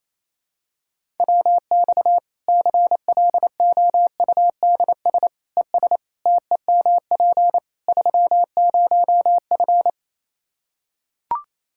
Morse code, heard as a keyed tone, about 27 wpm, 700 Hz. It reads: WX CLOUDS ES TEMP 30F
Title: K1USN SST QSO Practice 28wpm